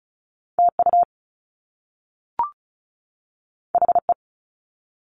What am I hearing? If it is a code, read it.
Morse code, heard as a keyed tone, about 36 wpm, 700 Hz. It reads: TU HE
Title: Top Two Letter Combinations in Words 35wpm